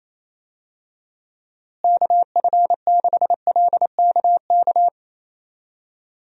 Morse code, heard as a keyed tone, about 28 wpm, 700 Hz. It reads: KF6LKK
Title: USA Call Signs - 6 Characters - Repeated 2x - 28wpm